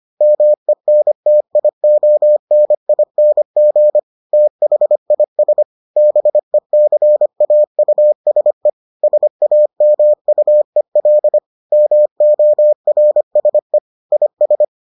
Morse code, heard as a keyed tone, about 25 wpm, 600 Hz. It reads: MENTIONING THIS BECAUSE SAMUEL MORSE IS